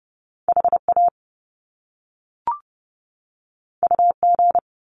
Morse code, heard as a keyed tone, about 30 wpm, 700 Hz. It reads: HA UG
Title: Top Two Letter Combinations in Words 30wpm